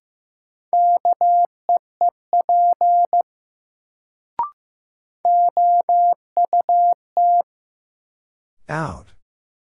Morse code, heard as a keyed tone, about 15 wpm, 700 Hz. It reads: KEEP OUT
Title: Sets of 2 Words - New Word Drill - 15wpm